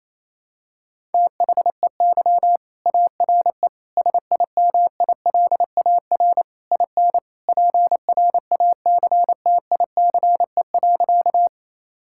Morse code, heard as a keyed tone, about 28 wpm, 700 Hz. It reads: THEY ARE SIMILAR IN PRACTICE.
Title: Sentences from Top 1000 Words - Review Part 5 - 28wpm